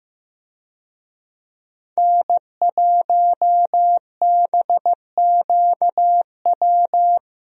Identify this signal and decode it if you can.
Morse code, heard as a keyed tone, about 15 wpm, 700 Hz. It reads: N1BQW
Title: USA Call Signs - 5 Characters - Repeated 2x - 15wpm